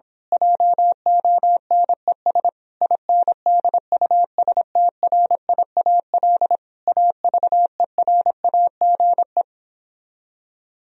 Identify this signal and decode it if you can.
Morse code, heard as a keyed tone, about 26 wpm, 700 Hz. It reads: JONES INDUSTRIAL AVERAGE
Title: Speed-Racing - Sets of 5 Words 22wpm